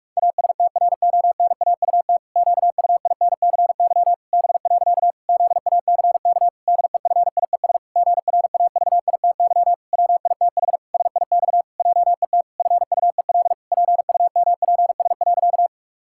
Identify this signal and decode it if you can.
Morse code, heard as a keyed tone, about 41 wpm, 700 Hz. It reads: ASTRONAUT QUINCY B. ZACK DEFIES GRAVITY WITH SIX JET FUEL PUMPS.
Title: Pangram Challenge v2 - 45wpm